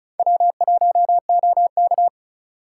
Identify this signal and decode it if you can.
Morse code, heard as a keyed tone, about 44 wpm, 700 Hz. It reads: W1OK